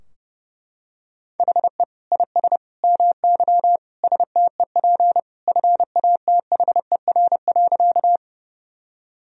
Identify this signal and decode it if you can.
Morse code, heard as a keyed tone, about 32 wpm, 700 Hz. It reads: HE IS MY STEP FATHER.
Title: Sentences from Top 1000 Words - Review Part 2 - 30wpm